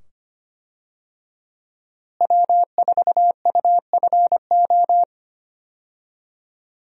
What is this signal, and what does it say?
Morse code, heard as a keyed tone, about 25 wpm, 700 Hz. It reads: W4UFO
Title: USA Call Signs - 5 Characters - Repeated 2x - 25wpm